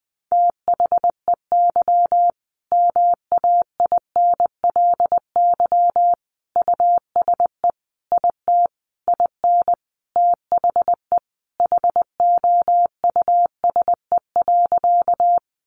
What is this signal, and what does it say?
Morse code, heard as a keyed tone, about 20 wpm, 700 Hz. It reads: THEY MAINLY USE IT IN THE HOUSE.
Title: Sentences from Top 1400 Words 20wpm